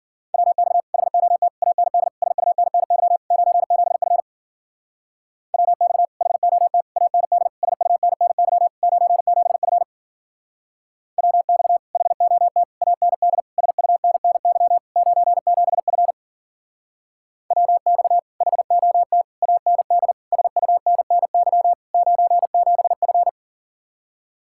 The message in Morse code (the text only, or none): WX HOT AND SUNNY 97F WX HOT AND SUNNY 97F WX HOT AND SUNNY 97F WX HOT AND SUNNY 97F